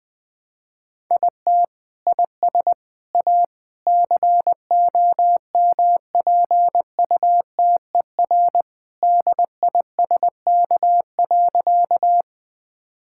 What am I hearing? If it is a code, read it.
Morse code, heard as a keyed tone, about 20 wpm, 700 Hz. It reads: IT IS A COMPUTER DISK.
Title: Sentences from Top 1000 Words - Review Part 6 - 20wpm